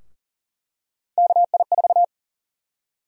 Morse code, heard as a keyed tone, about 40 wpm, 700 Hz. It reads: KI4
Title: USA Call Sign Prefixes - 40wpm